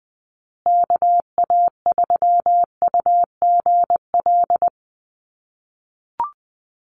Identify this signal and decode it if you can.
Morse code, heard as a keyed tone, about 20 wpm, 700 Hz. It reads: KA3UGL
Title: USA Call Signs Repeated 2x 20wpm